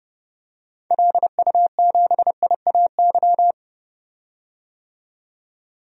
Morse code, heard as a keyed tone, about 30 wpm, 700 Hz. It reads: LU7IAY
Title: DXCC-100 Call Signs - 6 Characters - 30wpm